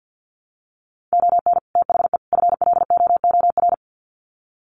Morse code, heard as a keyed tone, about 50 wpm, 700 Hz. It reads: ON THE FLOOR
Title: Sets of 3 Words 50wpm